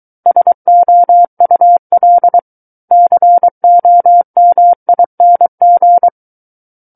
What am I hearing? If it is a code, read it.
Morse code, heard as a keyed tone, about 23 wpm, 700 Hz. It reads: SOUL COMING
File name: Book One / 14-Chapter 14 - 23 WPM